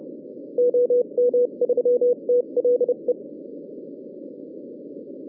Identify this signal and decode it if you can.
Morse code, heard as a keyed tone, about 33 wpm, spaced as for 29 wpm, 490 Hz. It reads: OM3TLE